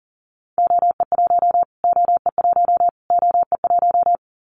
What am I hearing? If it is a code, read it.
Morse code, heard as a keyed tone, about 40 wpm, 700 Hz. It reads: OE1 OE1 OE1
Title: Callsign Prefixes for Top 25 DXCC - Repeated 3x - 40wpm